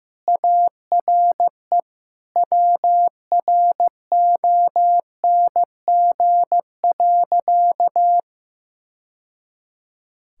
Morse code, heard as a keyed tone, about 15 wpm, 700 Hz. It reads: ARE WRONG.
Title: Sentences from Top 1000 Words - Review Part 5 - 15wpm